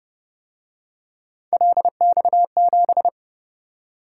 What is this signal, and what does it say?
Morse code, heard as a keyed tone, about 30 wpm, 700 Hz. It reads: LX7
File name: Speed-Racing - Three Letter-Number Combinations - 25wpm